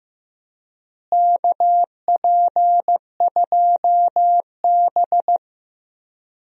Morse code, heard as a keyed tone, about 15 wpm, 700 Hz. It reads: KP2B
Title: DXCC-100 Call Signs - 4 Characters - Repeated 3x - 15wpm